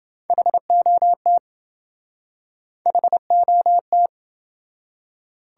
Morse code, heard as a keyed tone, about 27 wpm, 700 Hz. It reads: HOT HOT